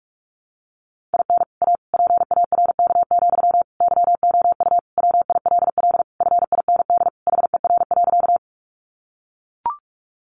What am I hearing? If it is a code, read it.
Morse code, heard as a keyed tone, about 44 wpm, 700 Hz. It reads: IN A PARK, YOU WILL FIND HER.
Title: Sentences from Top 1300 Words 45wpm